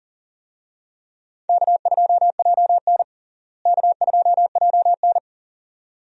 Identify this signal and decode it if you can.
Morse code, heard as a keyed tone, about 40 wpm, 700 Hz. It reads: K2JN K2JN
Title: USA Call Signs - 4 Characters - Repeated 2x - 40wpm